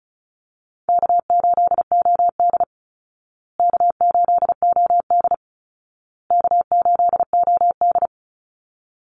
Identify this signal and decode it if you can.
Morse code, heard as a keyed tone, about 34 wpm, 700 Hz. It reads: K8OD K8OD K8OD
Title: Easy Call Signs Repeated 3x - 35wpm